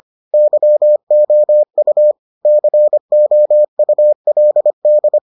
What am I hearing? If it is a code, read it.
Morse code, heard as a keyed tone, about 25 wpm, 600 Hz. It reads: YOU COULD